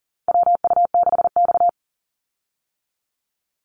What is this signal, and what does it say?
Morse code, heard as a keyed tone, about 40 wpm, 700 Hz. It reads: WU6X
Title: Call Signs - 40wpm